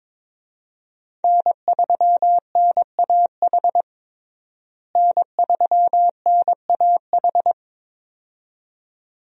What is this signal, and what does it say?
Morse code, heard as a keyed tone, about 22 wpm, 700 Hz. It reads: N3NAH N3NAH
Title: USA Call Signs - 5 Characters - Repeated 2x - 22wpm